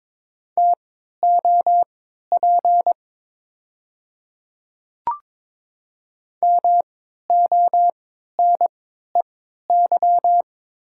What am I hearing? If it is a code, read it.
Morse code, heard as a keyed tone, about 23 wpm, spaced as for 13 wpm, 700 Hz. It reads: TOP MONEY
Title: 3x Character Spacing - Top 500 Words - 22wpm